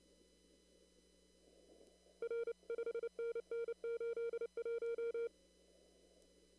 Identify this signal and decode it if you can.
Morse code, heard as a keyed tone, about 29 wpm, 460 Hz. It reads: R 5NN 81
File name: CW / kt5j wpx cw-op-k5pi